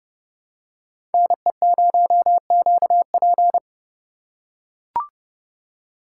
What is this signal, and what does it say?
Morse code, heard as a keyed tone, about 29 wpm, 700 Hz. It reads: NE0QP
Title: USA Call Signs - 5 Characters - 2x Character Spacing - 30wpm